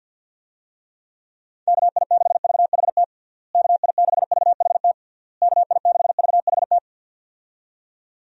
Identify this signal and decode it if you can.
Morse code, heard as a keyed tone, about 50 wpm, 700 Hz. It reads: KI6VHT KI6VHT KI6VHT
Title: USA Call Signs - 6 Characters - Repeated 3x - 50wpm